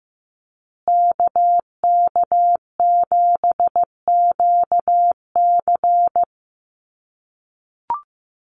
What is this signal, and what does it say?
Morse code, heard as a keyed tone, about 15 wpm, 700 Hz. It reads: KK7QC
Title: USA Call Signs Repeated 2x 15wpm